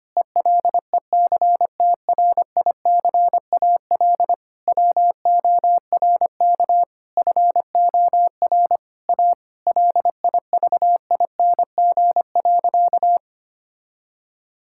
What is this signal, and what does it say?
Morse code, heard as a keyed tone, about 25 wpm, 700 Hz. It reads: ELECTRICAL WORK FOR A LIVING.
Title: Sentences from Top 1000 Words - Review Part 6 - 25wpm